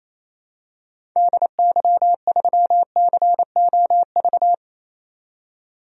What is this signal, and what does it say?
Morse code, heard as a keyed tone, about 28 wpm, 700 Hz. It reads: DY3COV
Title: DXCC-100 Call Signs - 6 Characters - 28wpm